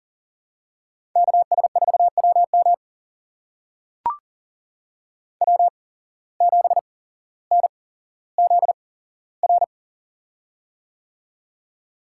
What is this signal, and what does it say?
Morse code, heard as a keyed tone, about 40 wpm, spaced as for 17 wpm, 700 Hz. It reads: KS4WM W7NZR
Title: USA Call Signs - 8x Character Spacing - 40wpm